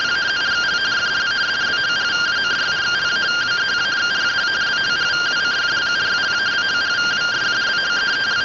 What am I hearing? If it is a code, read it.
Two-tone teleprinter signal, about 170 Hz shift.
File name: findings / CT 1 Scout 0001.gen-31 rtty rx